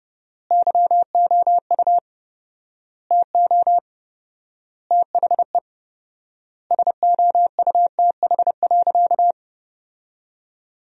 Morse code, heard as a keyed tone, about 30 wpm, 700 Hz. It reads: YOU TO THE SOUTH.
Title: 4x Word-Spacing - Sentences from Top 400 Words - 30wpm